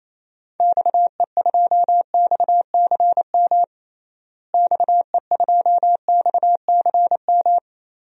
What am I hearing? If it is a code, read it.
Morse code, heard as a keyed tone, about 28 wpm, 700 Hz. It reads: XE2XCM XE2XCM
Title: DXCC-100 Call Signs - 6 Characters - Repeated 3x - 28wpm